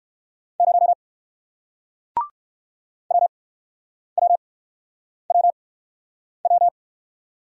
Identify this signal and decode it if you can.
Morse code, heard as a keyed tone, about 57 wpm, spaced as for 9 wpm, 700 Hz. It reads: 0 WWWW